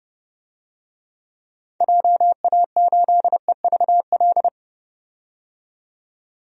Morse code, heard as a keyed tone, about 31 wpm, 700 Hz. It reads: JA8EVL
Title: DXCC-100 Call Signs - 6 Characters - 30wpm